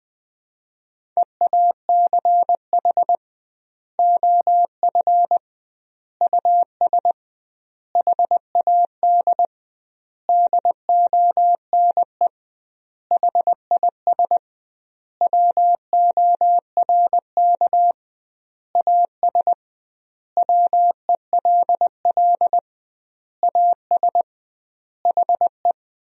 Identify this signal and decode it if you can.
Morse code, heard as a keyed tone, about 21 wpm, 700 Hz. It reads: EACH OF US HAD DONE HIS WORK AS WELL AS HE
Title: 2x Word-Spacing - Sentences from Top 500 Words - 20wpm